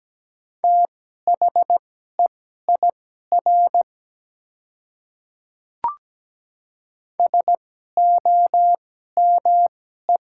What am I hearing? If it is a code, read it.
Morse code, heard as a keyed tone, about 17 wpm, spaced as for 12 wpm, 700 Hz. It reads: THEIR SOME
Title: 2x Character Spacing - Top 200 Words - 17wpm